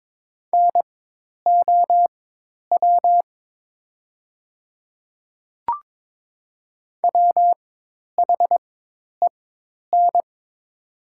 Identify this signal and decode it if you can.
Morse code, heard as a keyed tone, about 22 wpm, spaced as for 10 wpm, 700 Hz. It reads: NOW WHEN
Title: 4x Character Spacing - Top 100 Words - 22wpm